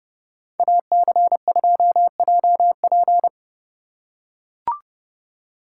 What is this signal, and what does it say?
Morse code, heard as a keyed tone, about 30 wpm, 700 Hz. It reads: AC2JP